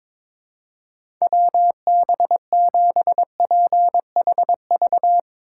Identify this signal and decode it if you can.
Morse code, heard as a keyed tone, about 22 wpm, 700 Hz. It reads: WB7PHV
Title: Speed Racing - USA 2x3 Call Signs 22wpm